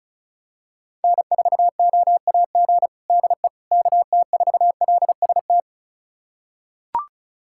Morse code, heard as a keyed tone, about 34 wpm, 700 Hz. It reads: N4OAG DE KT4LST
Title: USA Callsign DE Callsign 35wpm